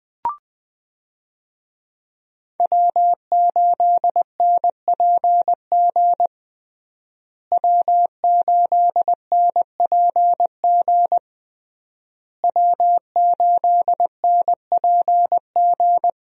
Morse code, heard as a keyed tone, about 21 wpm, 700 Hz. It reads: W8NPG W8NPG W8NPG
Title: USA Call Signs - 5 Characters - Repeated 3x - 20wpm